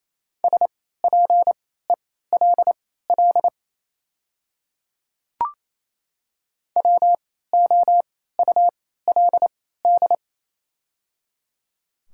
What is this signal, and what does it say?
Morse code, heard as a keyed tone, about 27 wpm, spaced as for 16 wpm, 690 Hz. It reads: SPELL WOULD
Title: 3x Character Spacing - Top 200 Words - 28wpm